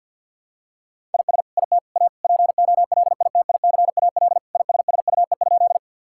Morse code, heard as a keyed tone, about 50 wpm, 700 Hz. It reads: IS IT A POLITICAL ISSUE?
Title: Sentences from Top 1000 Words - Review Part 6 - 50wpm